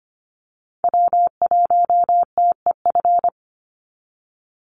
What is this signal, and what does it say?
Morse code, heard as a keyed tone, about 24 wpm, 700 Hz. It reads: W1TEF
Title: Call Signs Repeated 2x - 25wpm